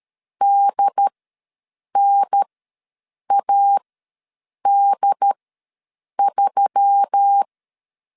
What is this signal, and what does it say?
Morse code, heard as a keyed tone, about 13 wpm, spaced as for 7 wpm, 800 Hz. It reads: DNAD3
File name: LEZIONI / 6 PKX0 / 6 PKX ws 12wpm 3106